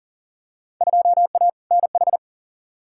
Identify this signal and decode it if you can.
Morse code, heard as a keyed tone, about 40 wpm, 700 Hz. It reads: 2A NH
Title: Field Day Exchanges - 40wpm